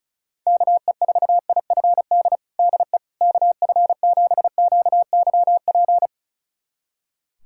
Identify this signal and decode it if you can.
Morse code, heard as a keyed tone, about 36 wpm, 700 Hz. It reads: KE4IFD DE KF7QYP
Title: USA Callsign DE Callsign - 6 Characters - 35wpm